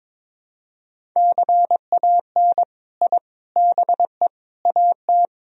Morse code, heard as a keyed tone, about 22 wpm, 700 Hz. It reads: CAN I BE AT
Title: Sentences from Top 1000 Words - Review Part 1 - 22wpm